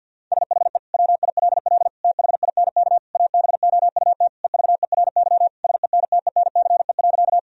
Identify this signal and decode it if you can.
Morse code, heard as a keyed tone, about 52 wpm, 700 Hz. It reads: SHE WILL THINK ABOUT EVERY SENTENCE.
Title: Sentences from Top 1000 Words - Review Part 1 - 50wpm